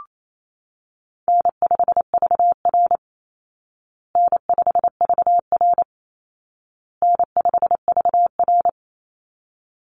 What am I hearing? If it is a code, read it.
Morse code, heard as a keyed tone, about 28 wpm, 700 Hz. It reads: N5VR N5VR N5VR
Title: Easy Call Signs Repeated 3x - 28wpm